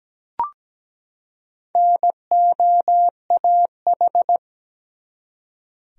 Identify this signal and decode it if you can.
Morse code, heard as a keyed tone, about 17 wpm, 700 Hz. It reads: NOAH
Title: USA Common Names - Max 4 Letters - 17wpm